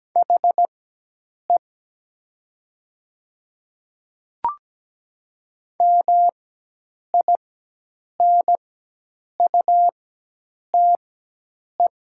Morse code, heard as a keyed tone, about 17 wpm, spaced as for 8 wpm, 700 Hz. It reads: HE MINUTE
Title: 4x Character Spacing - Top 500 Words - 17wpm